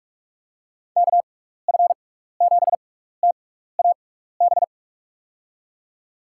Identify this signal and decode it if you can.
Morse code, heard as a keyed tone, about 46 wpm, spaced as for 15 wpm, 700 Hz. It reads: KF7TAB
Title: USA Call Signs - 6 Characters - 6x Character Spacing - 45wpm